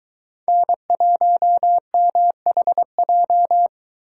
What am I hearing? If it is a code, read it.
Morse code, heard as a keyed tone, about 22 wpm, 700 Hz. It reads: N1MHJ